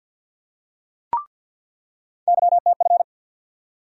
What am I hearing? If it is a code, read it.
Morse code, heard as a keyed tone, about 47 wpm, 700 Hz. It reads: YTF
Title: USA Call Sign Postfixes - 3 Letters 50wpm